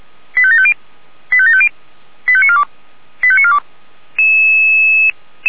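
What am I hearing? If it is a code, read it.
Five-tone selective call (ZVEI), heard as quick runs of five tones, tones of about 70 ms. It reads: 85750 85750 85942 85942
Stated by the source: ZVEI five-tone call received by SDR, from the feuerwehr-orlamuende.org fire brigade server.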